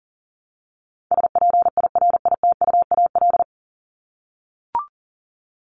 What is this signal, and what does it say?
Morse code, heard as a keyed tone, about 40 wpm, 700 Hz. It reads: SPIRITUAL
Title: Top 1501-1750 Words 40wpm